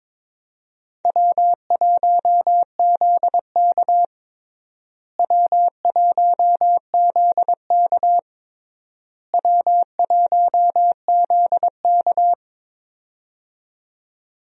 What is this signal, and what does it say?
Morse code, heard as a keyed tone, about 22 wpm, 700 Hz. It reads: W1ZK W1ZK W1ZK
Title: USA Call Signs - 4 Characters - Repeated 3x - 22wpm